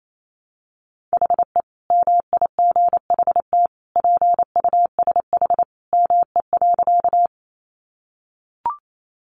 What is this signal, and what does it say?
Morse code, heard as a keyed tone, about 28 wpm, 700 Hz. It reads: HE MIGHT PUSH ME.
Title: Sentences from Top 1300 Words 28wpm